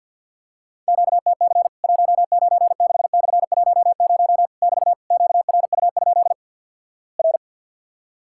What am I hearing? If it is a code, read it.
Morse code, heard as a keyed tone, about 50 wpm, 700 Hz. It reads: QTC 196/10 <BT> QRU? R
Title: Worked All Europe DX Contest - QTC Exchange with Repeats - 50wpm